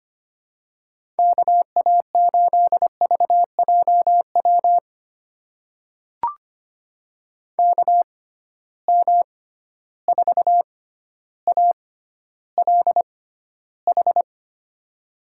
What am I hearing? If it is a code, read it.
Morse code, heard as a keyed tone, about 25 wpm, spaced as for 13 wpm, 700 Hz. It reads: KA8VJW KM4ALH
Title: USA Call Signs - 6 Characters - 6x Character Spacing - 25wpm